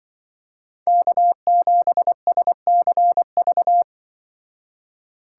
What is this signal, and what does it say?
Morse code, heard as a keyed tone, about 25 wpm, 700 Hz. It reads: K7SCV
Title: Speed Racing - USA 1x3 Call Signs 20wpm